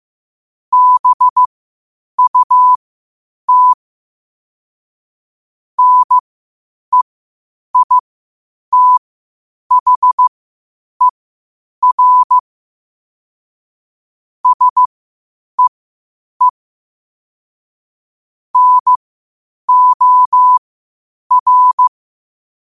Morse code, heard as a keyed tone, about 15 wpm, spaced as for 8 wpm, 1000 Hz. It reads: BUT NEITHER SEE NOR
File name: qotdfiles / 8wpm